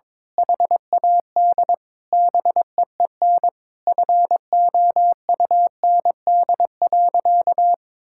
Morse code, heard as a keyed tone, about 21 wpm, 700 Hz. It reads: HAD BEEN FOUND.
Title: Sentences from Top 1600 Words 22wpm